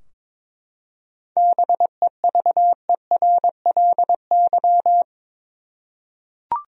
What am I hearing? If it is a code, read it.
Morse code, heard as a keyed tone, about 22 wpm, 700 Hz. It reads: BEVERLY
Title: USA Common Names - 22wpm